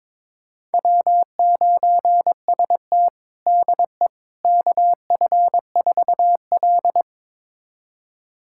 Morse code, heard as a keyed tone, about 23 wpm, 700 Hz. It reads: W9ST DE KF4L
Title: USA Callsign DE Callsign - 4 Characters - 22wpm